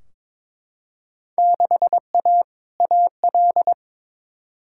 Morse code, heard as a keyed tone, about 22 wpm, 700 Hz. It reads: 6A AL
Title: Field Day Exchanges Encore - 22wpm